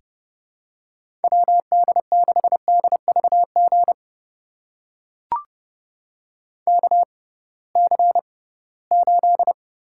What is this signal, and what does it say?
Morse code, heard as a keyed tone, about 30 wpm, spaced as for 19 wpm, 700 Hz. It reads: WD6DVG KC8
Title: USA Call Signs - 6 Characters - 6x Character Spacing - 30wpm